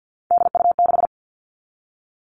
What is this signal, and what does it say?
Morse code, heard as a keyed tone, about 50 wpm, 700 Hz. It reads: DU6
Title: Three Challenging Letter-Number Combinations 50wpm